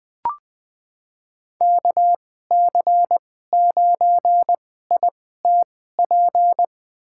Morse code, heard as a keyed tone, about 20 wpm, spaced as for 14 wpm, 700 Hz. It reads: KC9ITP
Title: USA Call Signs - 6 Characters - 2x Character Spacing - 20wpm